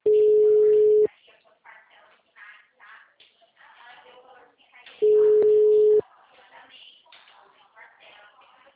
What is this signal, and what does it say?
Ringback tone (Europe, 425 Hz), with speech.